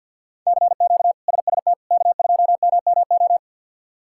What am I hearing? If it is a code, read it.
Morse code, heard as a keyed tone, about 47 wpm, 690 Hz. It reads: CQ SST K2MMO
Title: K1USN SST QSO Practice - 50wpm